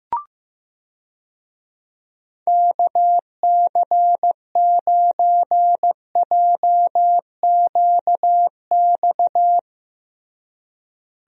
Morse code, heard as a keyed tone, about 14 wpm, 700 Hz. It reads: KC9JQX
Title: USA Call Signs - 6 Characters - Repeated 3x - 15wpm